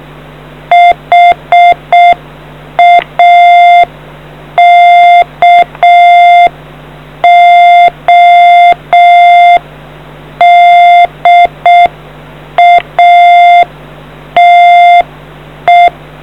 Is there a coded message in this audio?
Yes: Morse code from 0.7 to 15.9 s, heard as a keyed tone, about 6 wpm, 700 Hz. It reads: HAKODATE